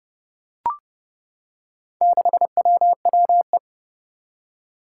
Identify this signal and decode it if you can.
Morse code, heard as a keyed tone, about 30 wpm, 700 Hz. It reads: 6WWE
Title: Four Challenging Letter-Number Combinations 30wpm